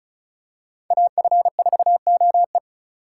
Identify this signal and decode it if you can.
Morse code, heard as a keyed tone, about 34 wpm, 700 Hz. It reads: AF4OE
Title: USA Call Signs - 4x Character Spacing - 35wpm